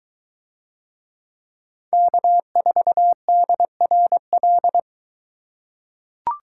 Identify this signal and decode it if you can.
Morse code, heard as a keyed tone, about 20 wpm, 700 Hz. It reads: K4DRL